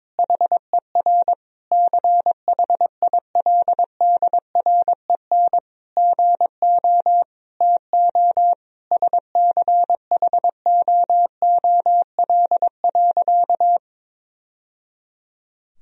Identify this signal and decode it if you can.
Morse code, heard as a keyed tone, about 22 wpm, 700 Hz. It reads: HER CHILDREN GO TO SCHOOL.
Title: Sentences from Top 1000 Words - Review Part 4 - 22wpm